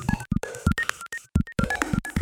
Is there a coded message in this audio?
No.